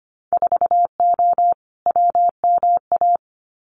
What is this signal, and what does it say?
Morse code, heard as a keyed tone, about 25 wpm, 700 Hz. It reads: 4O WMA
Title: Winter Field Day Exchanges Encore 25wpm